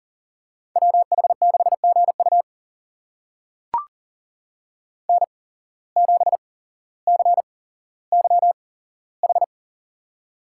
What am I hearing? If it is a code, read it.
Morse code, heard as a keyed tone, about 39 wpm, spaced as for 17 wpm, 700 Hz. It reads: WH6GU N7CYH
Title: USA Call Signs - 5 Characters - 8x Character Spacing - 40wpm